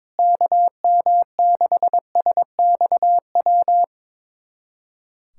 Morse code, heard as a keyed tone, about 22 wpm, 700 Hz. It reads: KM6SXW
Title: USA Callsign DE Callsign 22wpm